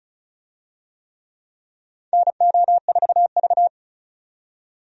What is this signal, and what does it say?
Morse code, heard as a keyed tone, about 36 wpm, 700 Hz. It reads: NO4V
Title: USA Call Signs - 4 Characters - 35wpm